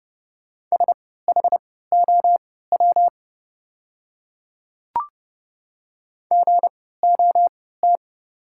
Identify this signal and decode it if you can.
Morse code, heard as a keyed tone, about 30 wpm, spaced as for 17 wpm, 690 Hz. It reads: SHOW GOT
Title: 3x Character Spacing - Top 500 Words - 30wpm